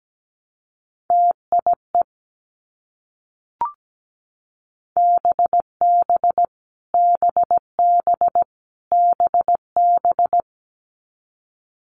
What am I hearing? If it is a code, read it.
Morse code, heard as a keyed tone, about 17 wpm, 700 Hz. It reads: TIE BB BB BB
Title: Multi-letter Phonemes Repeated 3x 17wpm